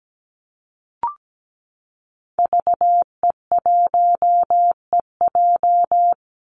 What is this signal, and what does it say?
Morse code, heard as a keyed tone, about 17 wpm, 700 Hz. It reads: VE1EJ
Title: Call Signs Repeated 3x - 17wpm